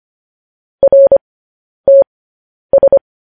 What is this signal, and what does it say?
Morse code, heard as a keyed tone, about 25 wpm, spaced as for 10 wpm, 550 Hz. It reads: RTS
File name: podcast / News headlines-1x-25x10